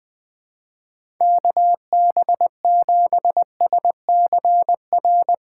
Morse code, heard as a keyed tone, about 20 wpm, 700 Hz. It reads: KB7SCR
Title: USA Call Signs - 3x Character Spacing - 20wpm